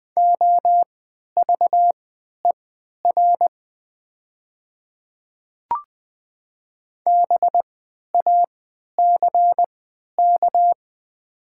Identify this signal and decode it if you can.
Morse code, heard as a keyed tone, about 20 wpm, spaced as for 11 wpm, 700 Hz. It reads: OVER BACK
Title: 3x Character Spacing - Top 200 Words - 20wpm